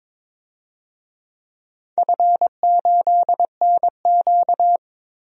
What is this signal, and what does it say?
Morse code, heard as a keyed tone, about 22 wpm, 700 Hz. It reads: F8NQ
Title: DXCC-100 Call Signs - 4 Characters - 22wpm